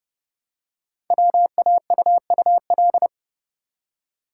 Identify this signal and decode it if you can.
Morse code, heard as a keyed tone, about 30 wpm, 690 Hz. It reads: WAUUL
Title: Code Groups - 5 Letters - 30wpm